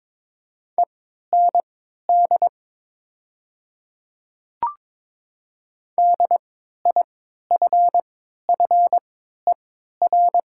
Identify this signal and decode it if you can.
Morse code, heard as a keyed tone, about 21 wpm, spaced as for 12 wpm, 700 Hz. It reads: END DIFFER
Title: 3x Character Spacing - Top 300 Words - 22wpm